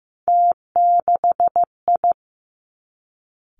Morse code, heard as a keyed tone, about 15 wpm, 700 Hz. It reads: T6I
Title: Three Challenging Letter-Number Combinations 15wpm